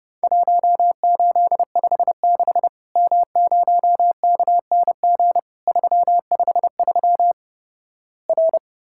Morse code, heard as a keyed tone, about 32 wpm, 700 Hz. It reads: 1856 M0KNG 353 R
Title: Worked All Europe DX Contest - QTC Exchange - 30wpm